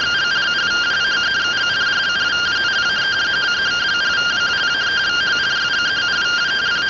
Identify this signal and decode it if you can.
Two-tone teleprinter signal, about 170 Hz shift.